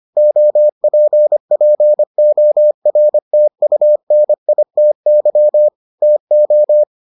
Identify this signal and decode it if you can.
Morse code, heard as a keyed tone, about 25 wpm, 600 Hz. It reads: OPPORTUNITY TO